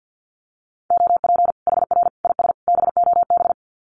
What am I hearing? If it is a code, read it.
Morse code, heard as a keyed tone, about 50 wpm, 700 Hz. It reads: OP HR IS BOB
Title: Common QSO Phrases 50wpm